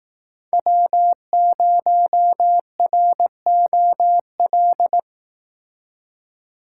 Morse code, heard as a keyed tone, about 18 wpm, 700 Hz. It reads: W0ROL
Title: Speed Racing - USA 1x3 Call Signs 15wpm